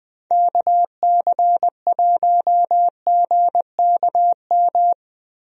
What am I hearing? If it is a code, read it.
Morse code, heard as a keyed tone, about 20 wpm, 700 Hz. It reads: KC1GKM
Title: USA Call Signs - 6 Characters - Repeated 2x - 20wpm